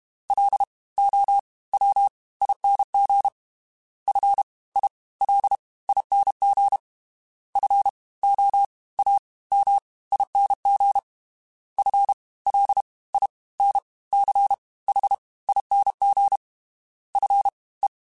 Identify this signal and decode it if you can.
Morse code, heard as a keyed tone, about 32 wpm, 800 Hz. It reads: L O W ING F I L ING F O A M ING F L I N C H ING F E
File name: Special INGs / Special INGs-mp3s / Fast ing @30